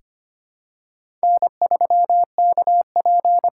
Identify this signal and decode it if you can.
Morse code, heard as a keyed tone, about 25 wpm, 700 Hz. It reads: N3KP
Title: USA Call Signs - 5 Characters - 8x Character Spacing - 25wpm